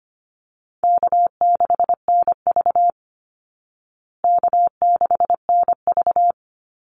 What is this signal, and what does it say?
Morse code, heard as a keyed tone, about 25 wpm, 700 Hz. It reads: K6NV K6NV
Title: Easy Call Signs Repeated 3x - 25wpm